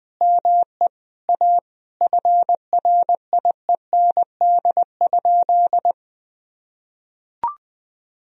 Morse code, heard as a keyed tone, about 20 wpm, 700 Hz. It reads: ME A FRIEND?
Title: Sentences from Top 1000 Words - Review Part 4 - 20wpm